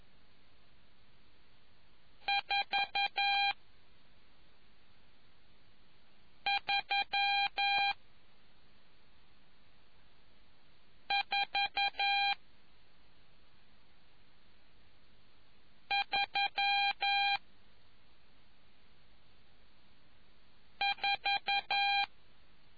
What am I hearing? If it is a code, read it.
Morse code, heard as a keyed tone, about 11 wpm, spaced as for 2 wpm, 800 Hz. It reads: 43434